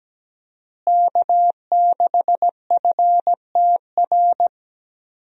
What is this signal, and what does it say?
Morse code, heard as a keyed tone, about 17 wpm, 700 Hz. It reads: K6FTR
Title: USA Call Signs - 5 Characters - Repeated 3x - 17wpm